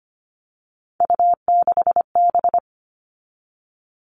Morse code, heard as a keyed tone, about 25 wpm, 700 Hz. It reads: U6B
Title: All Cut Numbers - 3-Digit 25wpm